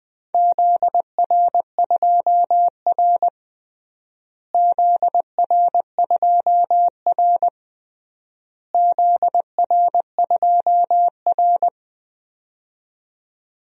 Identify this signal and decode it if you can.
Morse code, heard as a keyed tone, about 20 wpm, 700 Hz. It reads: ZR2R ZR2R ZR2R
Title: DXCC-100 Call Signs - 4 Characters - Repeated 3x - 20wpm